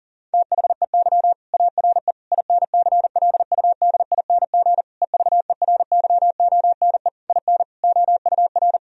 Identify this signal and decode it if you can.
Morse code, heard as a keyed tone, about 40 wpm, 700 Hz. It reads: THEY ARE INCLUDING EVERYONE IN OUR
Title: Sentences from Top 1000 Words - Review Part 4 - 40wpm